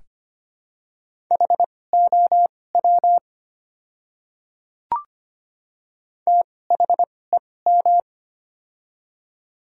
Morse code, heard as a keyed tone, about 25 wpm, spaced as for 18 wpm, 700 Hz. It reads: HOW THEM
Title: 2x Character Spacing - Top 100 Words - 25wpm